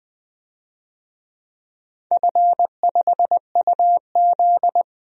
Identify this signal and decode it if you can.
Morse code, heard as a keyed tone, about 20 wpm, 700 Hz. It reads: F5UZ
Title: DXCC-100 Call Signs - 4 Characters - 20wpm